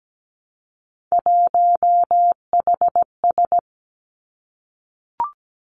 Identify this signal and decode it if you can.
Morse code, heard as a keyed tone, about 17 wpm, 700 Hz. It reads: 1HS
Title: Three Challenging Letter-Number Combinations 17wpm